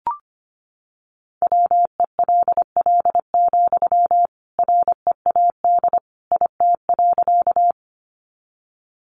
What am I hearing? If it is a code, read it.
Morse code, heard as a keyed tone, about 25 wpm, 700 Hz. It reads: WELL, READ IT.
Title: Sentences from Top 200 Words 25wpm